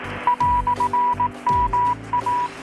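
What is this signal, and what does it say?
Morse code, heard as a keyed tone, about 18 wpm, 980 Hz. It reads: "MA
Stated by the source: Morse from KPH's Night of Nights XX maritime radiotelegraphy event, received in Chiapas, Mexico about 3,800 km away on a Kenwood R-600 with a grounded 10 m long-wire antenna.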